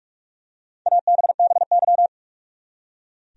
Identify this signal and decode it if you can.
Morse code, heard as a keyed tone, about 45 wpm, 700 Hz. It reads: ABBY
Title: USA Common Names - 45wpm